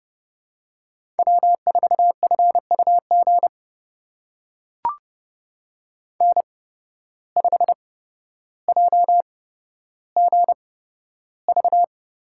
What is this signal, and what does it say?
Morse code, heard as a keyed tone, about 30 wpm, spaced as for 13 wpm, 700 Hz. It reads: W4FUG N5JGV
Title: USA Call Signs - 5 Characters - 8x Character Spacing - 30wpm